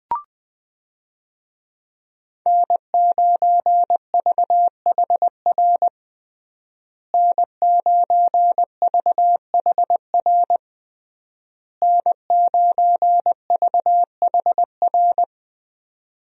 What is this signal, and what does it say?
Morse code, heard as a keyed tone, about 20 wpm, 700 Hz. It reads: N9VHR N9VHR N9VHR
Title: USA Call Signs - 5 Characters - Repeated 3x - 20wpm